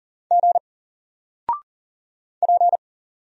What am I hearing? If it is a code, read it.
Morse code, heard as a keyed tone, about 40 wpm, 700 Hz. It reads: G P
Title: Single Character (Letter:Number:Punctuation:Prosign) 40wpm